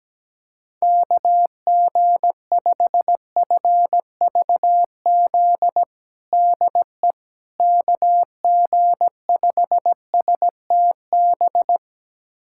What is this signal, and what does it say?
Morse code, heard as a keyed tone, about 17 wpm, 700 Hz. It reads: KG5FVZ DE KG5STB
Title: USA Callsign DE Callsign - 6 Characters - 17wpm